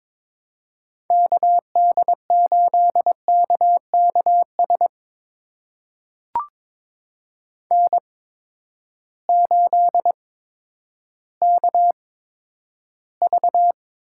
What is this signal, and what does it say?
Morse code, heard as a keyed tone, about 22 wpm, spaced as for 11 wpm, 700 Hz. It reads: KD8KKS N8KV
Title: USA Call Signs - 8x Character Spacing - 22wpm